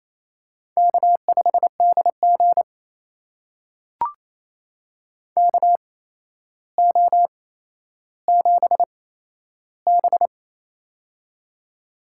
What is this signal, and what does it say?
Morse code, heard as a keyed tone, about 28 wpm, spaced as for 12 wpm, 700 Hz. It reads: K5DG KO7B
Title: USA Call Signs - 4 Characters - 8x Character Spacing - 28wpm